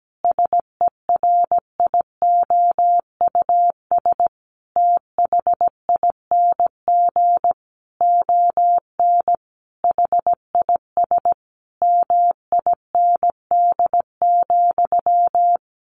Morse code, heard as a keyed tone, about 17 wpm, 700 Hz. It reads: SERIOUS THING ON HIS MIND,
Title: Sentences from Top 1100 Words 17wpm